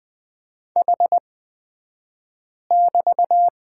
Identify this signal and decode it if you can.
Morse code, heard as a keyed tone, about 20 wpm, spaced as for 5 wpm, 700 Hz. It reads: H<BT>
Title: 242 Lesson 39d - Letter z - Two Characters 20wpm